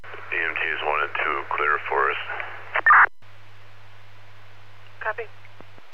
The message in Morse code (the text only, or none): none